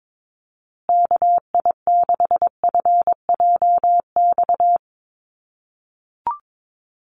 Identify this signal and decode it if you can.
Morse code, heard as a keyed tone, about 22 wpm, 700 Hz. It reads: KI6FJX
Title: USA Call Signs Repeated 3x 22wpm